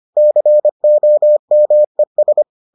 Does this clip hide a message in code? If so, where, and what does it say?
Yes: Morse code from 0.2 to 2.4 s, heard as a keyed tone, about 25 wpm, 600 Hz. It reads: COMES